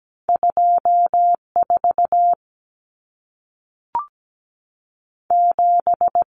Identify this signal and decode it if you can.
Morse code, heard as a keyed tone, about 17 wpm, 700 Hz. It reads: 24 7
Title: Numbers - 2-Digit 17wpm